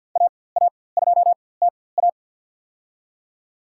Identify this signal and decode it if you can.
Morse code, heard as a keyed tone, about 50 wpm, spaced as for 23 wpm, 700 Hz. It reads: AA2TA
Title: USA Call Signs - 5 Characters - 4x Character Spacing - 50wpm